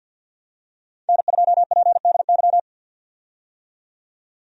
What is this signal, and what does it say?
Morse code, heard as a keyed tone, about 45 wpm, 700 Hz. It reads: N2PDY
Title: USA Call Signs - 5 Characters - 50wpm